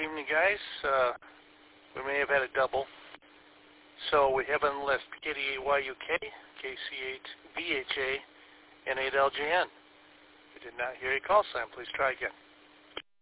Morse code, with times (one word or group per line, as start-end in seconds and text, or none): none